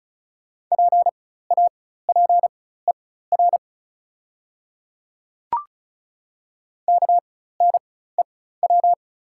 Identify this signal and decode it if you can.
Morse code, heard as a keyed tone, about 36 wpm, spaced as for 16 wpm, 700 Hz. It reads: PAPER KNEW
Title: 4x Character Spacing - Top 500 Words - 35wpm